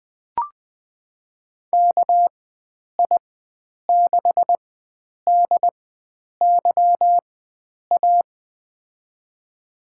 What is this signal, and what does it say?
Morse code, heard as a keyed tone, about 21 wpm, spaced as for 9 wpm, 700 Hz. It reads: KI6DYA
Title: USA Call Signs - 6 Characters - 4x Character Spacing - 20wpm